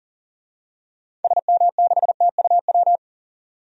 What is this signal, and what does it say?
Morse code, heard as a keyed tone, about 40 wpm, 700 Hz. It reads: SM6TUW
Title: DXCC-100 Call Signs - 6 Characters - 40wpm